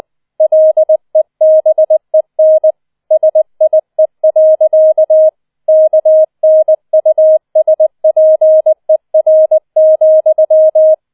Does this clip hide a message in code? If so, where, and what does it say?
Yes: Morse code from 0.4 to 10.9 s, heard as a keyed tone, about 20 wpm, 630 Hz. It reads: LEBEN SIE. KNUSPER,